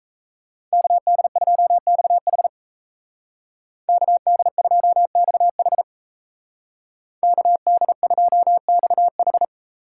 Morse code, heard as a keyed tone, about 35 wpm, 700 Hz. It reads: KD2XH KD2XH KD2XH